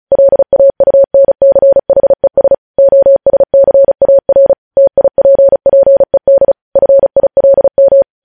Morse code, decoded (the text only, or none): LAUNCHES OSCAR TIPPED FILM